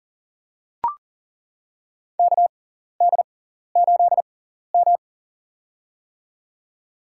Morse code, heard as a keyed tone, about 42 wpm, spaced as for 14 wpm, 700 Hz. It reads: KD8M
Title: USA Call Signs - 4 Characters - 6x Character Spacing - 40wpm